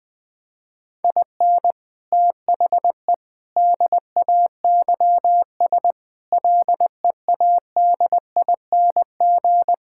Morse code, heard as a keyed tone, about 21 wpm, 700 Hz. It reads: IN THE DAYS LEADING
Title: Sets of 6 Words - 20wpm